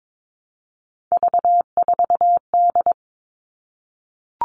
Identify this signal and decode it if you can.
Morse code, heard as a keyed tone, about 22 wpm, 700 Hz. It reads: V4D
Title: All Cut Numbers - 3-Digit 22wpm